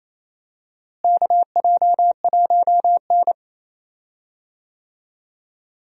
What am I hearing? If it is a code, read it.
Morse code, heard as a keyed tone, about 28 wpm, 700 Hz. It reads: KJ1N
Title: USA Call Signs - 4 Characters - Repeated 3x - 28wpm